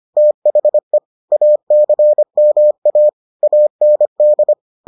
Morse code, heard as a keyed tone, about 25 wpm, 600 Hz. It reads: THE ACMA AND